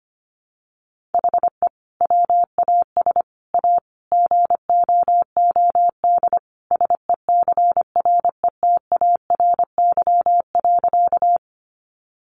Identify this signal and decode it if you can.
Morse code, heard as a keyed tone, about 25 wpm, 700 Hz. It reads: HE WAS A GOOD SECRETARY.
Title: Sentences from Top 1300 Words 25wpm